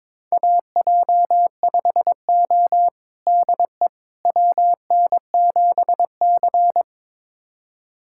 Morse code, heard as a keyed tone, about 22 wpm, 700 Hz. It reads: AJ5O DE WN7C
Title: USA Callsign DE Callsign - 4 Characters - 22wpm